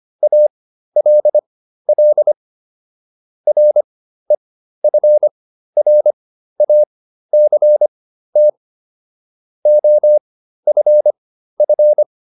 Morse code, heard as a keyed tone, about 25 wpm, spaced as for 13 wpm, 600 Hz. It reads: ALL REFRACT OFF